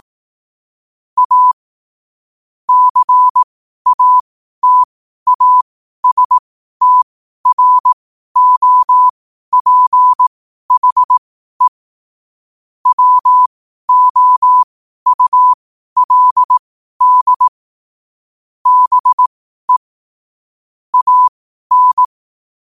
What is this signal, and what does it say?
Morse code, heard as a keyed tone, about 18 wpm, spaced as for 12 wpm, 1000 Hz. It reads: A CATASTROPHE WOULD BE AN
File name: qotdfiles / 12wpm